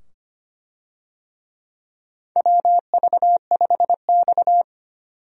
Morse code, heard as a keyed tone, about 25 wpm, 700 Hz. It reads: WV5X
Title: USA Call Signs - 4 Characters - 25wpm